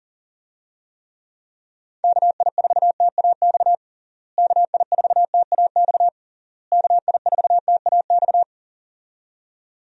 Morse code, heard as a keyed tone, about 40 wpm, 700 Hz. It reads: KI4TAX KI4TAX KI4TAX
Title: USA Call Signs - 6 Characters - Repeated 3x - 40wpm